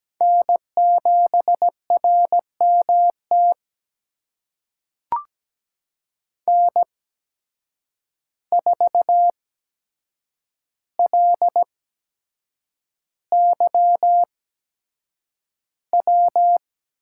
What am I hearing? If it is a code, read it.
Morse code, heard as a keyed tone, about 17 wpm, spaced as for 7 wpm, 700 Hz. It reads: N7RMT N4LYW
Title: USA Call Signs - 5 Characters - 8x Character Spacing - 17wpm